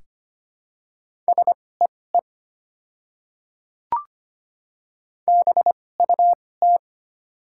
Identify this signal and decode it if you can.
Morse code, heard as a keyed tone, about 25 wpm, spaced as for 18 wpm, 700 Hz. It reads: SEE BUT
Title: 2x Character Spacing - Top 100 Words - 25wpm